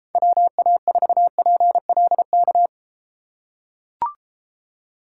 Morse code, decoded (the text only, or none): WA4PLK